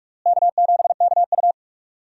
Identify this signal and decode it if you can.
Morse code, heard as a keyed tone, about 46 wpm, 700 Hz. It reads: K7KU
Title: USA Call Signs - 4 Characters - Repeated 2x - 45wpm